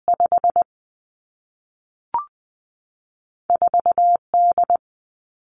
Morse code, heard as a keyed tone, about 20 wpm, 700 Hz. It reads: 5 4 D
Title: Two Challenging Letter-Number Combinations 20wpm